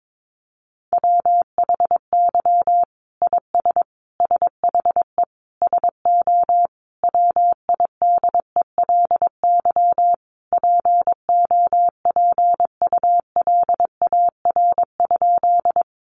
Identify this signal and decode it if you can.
Morse code, heard as a keyed tone, about 22 wpm, 700 Hz. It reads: WHY IS SHE SO WIDELY POPULAR?
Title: Sentences from Top 1200 Words 22wpm